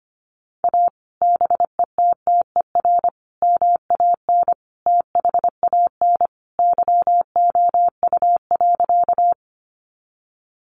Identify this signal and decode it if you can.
Morse code, heard as a keyed tone, about 25 wpm, 700 Hz. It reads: A BETTER MAN THAN YOU.
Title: Sentences from Top 400 Words 25wpm